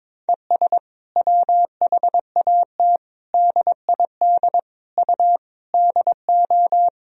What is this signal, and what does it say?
Morse code, heard as a keyed tone, about 22 wpm, 700 Hz. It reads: ES WHAT DID U DO
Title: Sentences from Top 100 Words and Abbreviations 22wpm